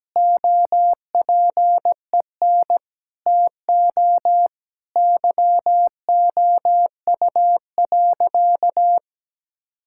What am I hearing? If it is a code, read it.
Morse code, heard as a keyed tone, about 17 wpm, 700 Hz. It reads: OPEN TO YOU.
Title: Sentences from Top 1000 Words - Review Part 6 - 17wpm